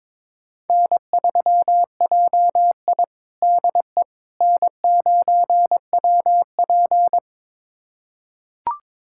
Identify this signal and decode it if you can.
Morse code, heard as a keyed tone, about 22 wpm, 700 Hz. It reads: N3JI DE N9WP
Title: USA Callsign DE Callsign - 4 Characters - 22wpm